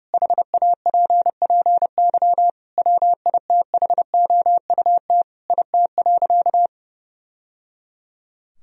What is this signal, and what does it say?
Morse code, heard as a keyed tone, about 30 wpm, 700 Hz. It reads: HAPPY WITHOUT IT.
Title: Sentences from Top 1000 Words - Review Part 3 - 30wpm